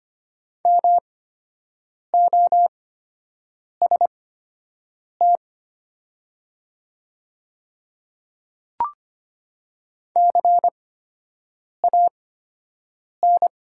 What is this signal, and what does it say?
Morse code, heard as a keyed tone, about 25 wpm, spaced as for 7 wpm, 700 Hz. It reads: MOST CAN
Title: 8x Character Spacing - Top 300 Words - 25wpm